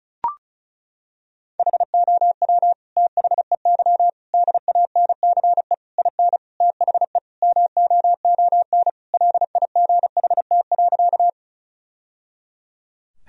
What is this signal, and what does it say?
Morse code, heard as a keyed tone, about 36 wpm, 700 Hz. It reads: HOW THEY DANCE IN THE MOON LIGHT.
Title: Sentences from Top 1000 Words - Review Part 4 - 35wpm